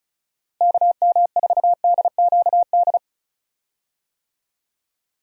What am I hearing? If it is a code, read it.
Morse code, heard as a keyed tone, about 34 wpm, 700 Hz. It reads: KM4DQD
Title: USA Call Signs - 6 Characters - Repeated 3x - 35wpm